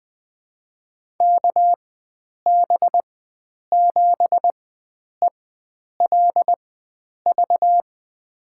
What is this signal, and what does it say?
Morse code, heard as a keyed tone, about 20 wpm, spaced as for 9 wpm, 700 Hz. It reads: KB7ELV
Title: USA Call Signs - 6 Characters - 4x Character Spacing - 20wpm